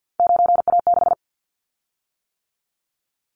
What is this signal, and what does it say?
Morse code, heard as a keyed tone, about 50 wpm, 700 Hz. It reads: 9A6
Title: Callsign Prefixes for Top 25 DXCC - 50wpm